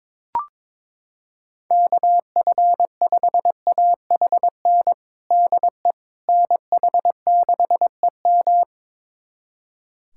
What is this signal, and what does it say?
Morse code, heard as a keyed tone, about 22 wpm, 700 Hz. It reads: KF5AHN DE NH6EM
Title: USA Callsign DE Callsign 22wpm